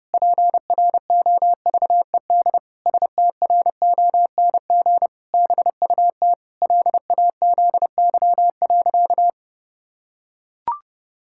Morse code, heard as a keyed tone, about 32 wpm, 700 Hz. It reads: PROVED STRONG BUT LAZY.
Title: Pangram Challenge v2 - 30wpm